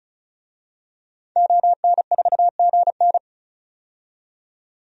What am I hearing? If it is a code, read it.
Morse code, heard as a keyed tone, about 34 wpm, 700 Hz. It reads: ON4GN
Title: DXCC-100 Call Signs - 5 Characters - 35wpm